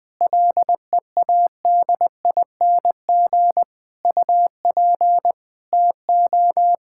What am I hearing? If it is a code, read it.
Morse code, heard as a keyed tone, about 20 wpm, 700 Hz. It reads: LEADING UP TO
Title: Sets of 6 Words - 20wpm